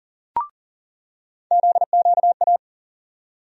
Morse code, heard as a keyed tone, about 40 wpm, 700 Hz. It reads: ZQA
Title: USA Call Sign Postfixes - 3 Letters 40wpm